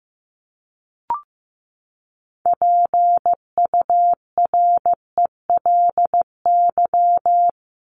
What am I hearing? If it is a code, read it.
Morse code, heard as a keyed tone, about 15 wpm, 700 Hz. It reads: PURELY
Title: LY Suffix - Top 100 Words with Base Words 15wpm